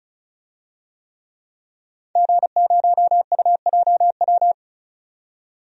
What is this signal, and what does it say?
Morse code, heard as a keyed tone, about 34 wpm, 700 Hz. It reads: G0UJW
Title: DXCC-100 Call Signs - 5 Characters - 35wpm